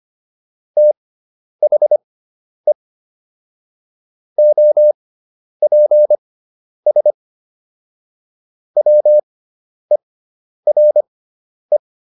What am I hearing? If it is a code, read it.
Morse code, heard as a keyed tone, about 25 wpm, spaced as for 10 wpm, 600 Hz. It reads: THE OPS WERE